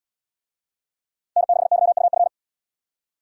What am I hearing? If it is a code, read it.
Morse code, heard as a keyed tone, about 66 wpm, 700 Hz. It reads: N6YUF